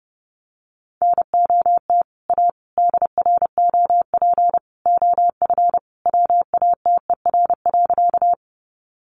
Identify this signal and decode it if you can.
Morse code, heard as a keyed tone, about 30 wpm, 700 Hz. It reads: NOT A DROP OF WATER.
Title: Sentences from Top 1400 Words 30wpm